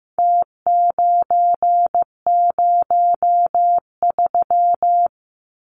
Morse code, heard as a keyed tone, about 15 wpm, 700 Hz. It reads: T903
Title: Numbers - Contest Style - 4-Digit 15wpm